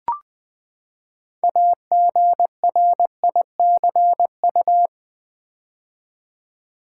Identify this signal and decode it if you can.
Morse code, heard as a keyed tone, about 20 wpm, 700 Hz. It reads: AGRICU
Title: Build A Word - 6 Letters At A Time - 12-Letter Words - 20wpm